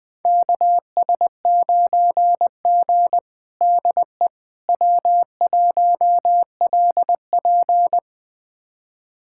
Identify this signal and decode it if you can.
Morse code, heard as a keyed tone, about 20 wpm, 700 Hz. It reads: KS9G DE W1LP
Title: USA Callsign DE Callsign - 4 Characters - 20wpm